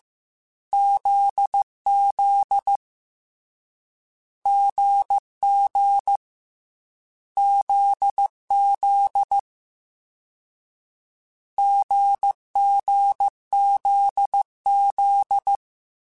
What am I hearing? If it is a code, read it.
Morse code, heard as a keyed tone, about 15 wpm, 800 Hz. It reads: ZZ GG ZZ GGZZ